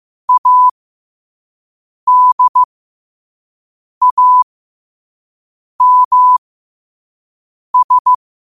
Morse code, heard as a keyed tone, about 15 wpm, spaced as for 5 wpm, 1000 Hz. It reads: ADAMS
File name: qotdfiles / 5wpm